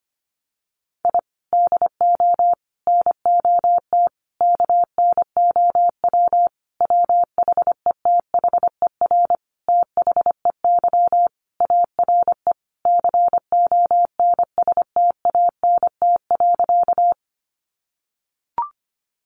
Morse code, heard as a keyed tone, about 25 wpm, 700 Hz. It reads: I DO NOT KNOW WHETHER THEY ARE CONSTANT.
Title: Sentences from Top 1500 Words 25wpm